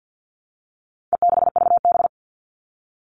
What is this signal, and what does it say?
Morse code, heard as a keyed tone, about 50 wpm, 700 Hz. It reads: E6VB
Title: All Cut Numbers - 4-Digit 50wpm